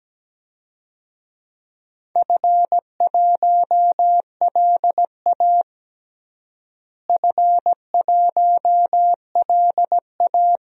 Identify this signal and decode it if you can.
Morse code, heard as a keyed tone, about 17 wpm, 700 Hz. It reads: F1LA F1LA
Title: DXCC-100 Call Signs - 4 Characters - Repeated 3x - 17wpm